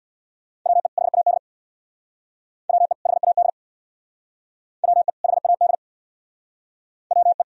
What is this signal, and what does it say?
Morse code, heard as a keyed tone, about 63 wpm, 700 Hz. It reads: WE HAD WE HAD WE HAD WE